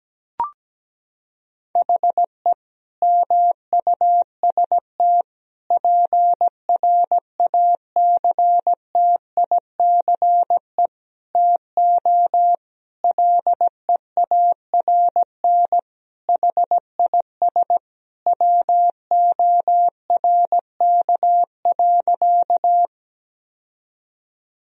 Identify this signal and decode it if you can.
Morse code, heard as a keyed tone, about 17 wpm, 700 Hz. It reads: HE MUST PRACTICE TO LEARN HIS WORK.
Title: Sentences from Top 1000 Words - Review Part 4 - 17wpm